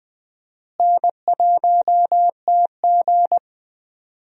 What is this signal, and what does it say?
Morse code, heard as a keyed tone, about 20 wpm, 700 Hz. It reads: N1TG
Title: USA Call Signs - 4 Characters - Repeated 3x - 20wpm